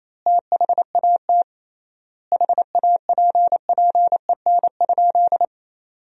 Morse code, heard as a keyed tone, about 28 wpm, 700 Hz. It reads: THAT HAPPEN?
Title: 3x Word-Spacing - Sentences from Top 400 Words - 28wpm